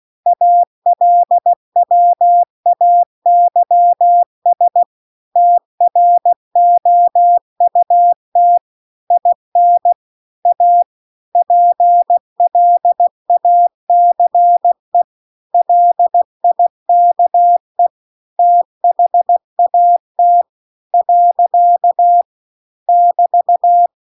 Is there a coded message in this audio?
Yes: Morse code from 0.3 to 23.9 s, heard as a keyed tone, about 16 wpm, 700 Hz. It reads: ALWAYS TROUT IN A PLACE LIKE THAT . <BT>